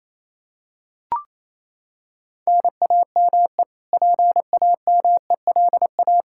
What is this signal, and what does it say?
Morse code, heard as a keyed tone, about 28 wpm, 700 Hz. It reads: NAME PAMELA
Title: Practice QSO - Exchange 1 - Beginning - 28wpm